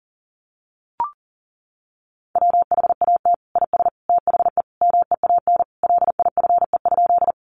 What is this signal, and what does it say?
Morse code, heard as a keyed tone, about 40 wpm, 700 Hz. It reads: WHAT IS THE MEAN LIFE?
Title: Sentences from Top 900 Words 40wpm